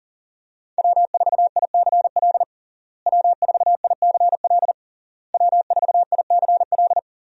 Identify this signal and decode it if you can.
Morse code, heard as a keyed tone, about 40 wpm, 700 Hz. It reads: W4ICL W4ICL W4ICL
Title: USA Call Signs - 5 Characters - Repeated 3x - 40wpm